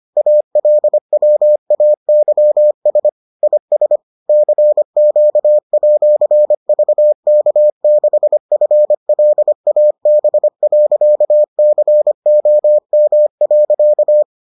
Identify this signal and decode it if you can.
Morse code, heard as a keyed tone, about 25 wpm, 600 Hz. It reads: ALWAYS IS CQ@VK6FLAB.COM.